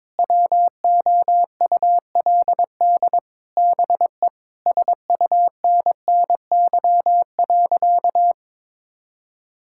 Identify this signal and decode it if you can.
Morse code, heard as a keyed tone, about 22 wpm, 700 Hz. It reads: WOULD BE SUNNY.
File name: practice / 22wpm / Sentences with 4-Word Phrases - 22wpm